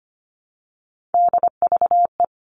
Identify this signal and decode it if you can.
Morse code, heard as a keyed tone, about 25 wpm, 700 Hz. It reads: DVE
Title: All Cut Numbers - 3-Digit 25wpm